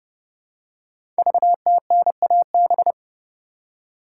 Morse code, heard as a keyed tone, about 30 wpm, 700 Hz. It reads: VTNAB
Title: Code Groups - 5 Letters - 30wpm